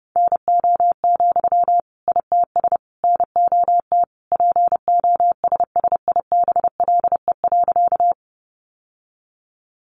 Morse code, heard as a keyed tone, about 30 wpm, 700 Hz. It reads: NO, ITS NOT POSSIBLE.
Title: Sentences from Top 600 Words 30wpm